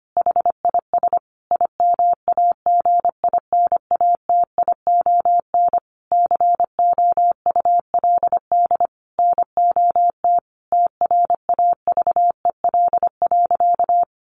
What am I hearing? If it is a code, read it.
Morse code, heard as a keyed tone, about 25 wpm, 700 Hz. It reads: HIS IMAGINATION COULD NOT TRAVEL.
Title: Sentences from Top 1300 Words 25wpm